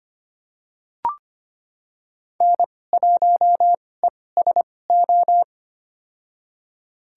Morse code, heard as a keyed tone, about 25 wpm, spaced as for 18 wpm, 700 Hz. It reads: N1ESO
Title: USA Call Signs - 5 Characters - 2x Character Spacing - 25wpm